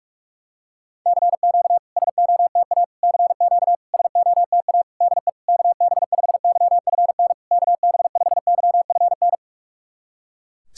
Morse code, heard as a keyed tone, about 44 wpm, 690 Hz. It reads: CQ SOTA CQ SOTA DE KB5YFN KB5YFN
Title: SOTA QSO Practice 45wpm